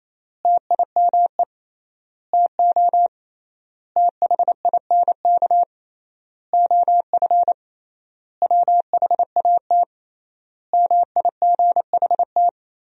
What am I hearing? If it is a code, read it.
Morse code, heard as a keyed tone, about 28 wpm, 700 Hz. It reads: TIME TO THINK OF WHAT MIGHT
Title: 3x Word-Spacing - Sentences from Top 400 Words - 28wpm